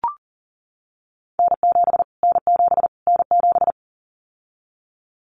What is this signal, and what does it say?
Morse code, heard as a keyed tone, about 40 wpm, 700 Hz. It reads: N7 N7 N7
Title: Callsign Prefixes for Top 25 DXCC - Repeated 3x - 40wpm